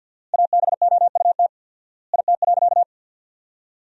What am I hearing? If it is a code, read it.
Morse code, heard as a keyed tone, about 52 wpm, 700 Hz. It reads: ABOUT IT.
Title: 4x Word-Spacing - Sentences from Top 400 Words - 50wpm